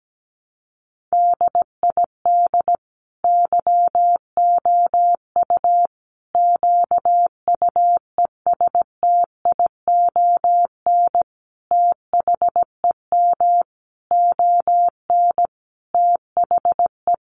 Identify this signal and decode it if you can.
Morse code, heard as a keyed tone, about 17 wpm, 700 Hz. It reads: DID YOU QUESTION THEM ON THE
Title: Sentences from Top 800 Words 17wpm